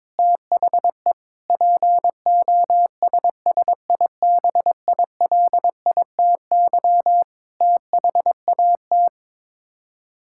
Morse code, heard as a keyed tone, about 22 wpm, 700 Hz. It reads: THE POSSIBILITY THAT
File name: practice / 22wpm / Sets of 8 Words - 22wpm